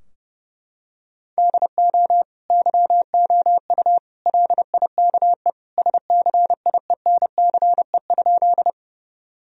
Morse code, heard as a keyed tone, about 30 wpm, 700 Hz. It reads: DO YOU LIKE SCIENCE?
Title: Sentences from Top 1000 Words - Review Part 2 - 30wpm